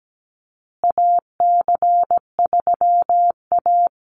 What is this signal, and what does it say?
Morse code, heard as a keyed tone, about 17 wpm, 700 Hz. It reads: AC3A
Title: USA Call Signs Repeated 3x 17wpm